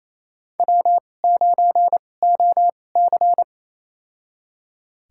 Morse code, heard as a keyed tone, about 28 wpm, spaced as for 20 wpm, 700 Hz. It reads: W9OC
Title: USA Call Signs - 4 Characters - 2x Character Spacing - 28wpm